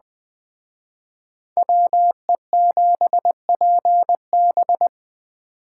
Morse code, heard as a keyed tone, about 20 wpm, 700 Hz. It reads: WE7PB
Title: Speed Racing - USA 2x2 Call Signs 17wpm